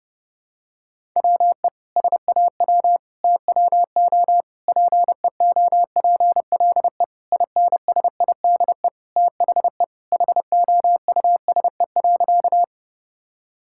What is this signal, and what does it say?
Morse code, heard as a keyed tone, about 30 wpm, 700 Hz. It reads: WE SAW TWO PEOPLE INSIDE THE HOUSE.
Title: Sentences from Top 1000 Words - Review Part 6 - 30wpm